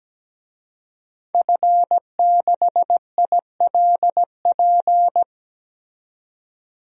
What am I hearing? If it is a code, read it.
Morse code, heard as a keyed tone, about 17 wpm, 700 Hz. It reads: F6ILP
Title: DXCC-100 Call Signs - 5 Characters - 17wpm